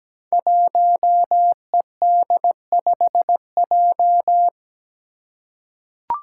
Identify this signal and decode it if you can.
Morse code, heard as a keyed tone, about 17 wpm, 700 Hz. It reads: 1ED5J
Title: Five Challenging Letter-Number Combinations 17wpm